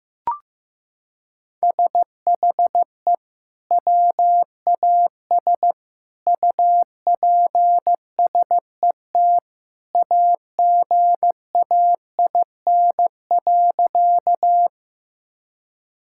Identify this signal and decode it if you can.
Morse code, heard as a keyed tone, about 15 wpm, 700 Hz. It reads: SHE WAS UPSET AGAIN.
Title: Sentences from Top 1000 Words - Review Part 5 - 15wpm